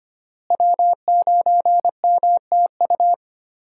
Morse code, heard as a keyed tone, about 25 wpm, 700 Hz. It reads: W9MTU
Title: USA Call Signs - 5 Characters - Repeated 3x - 25wpm